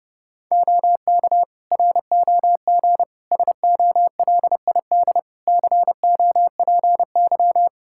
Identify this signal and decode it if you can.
Morse code, heard as a keyed tone, about 31 wpm, 700 Hz. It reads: OK ROG SOLID COPY
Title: Practice QSO - Exchange 3 - More Info - 30wpm